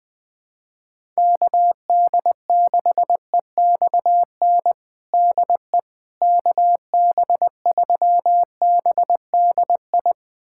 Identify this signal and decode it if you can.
Morse code, heard as a keyed tone, about 20 wpm, 700 Hz. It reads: KD6EXN DE KB3BDI
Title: USA Callsign DE Callsign 20wpm